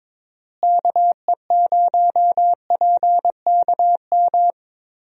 Morse code, heard as a keyed tone, about 22 wpm, 700 Hz. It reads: KE0PKM
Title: USA Call Signs - 6 Characters - Repeated 3x - 22wpm